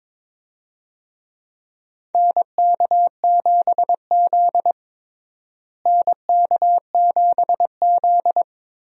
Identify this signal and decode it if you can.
Morse code, heard as a keyed tone, about 22 wpm, 700 Hz. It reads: NK7Z NK7Z
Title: USA Call Signs - 4 Characters - Repeated 3x - 22wpm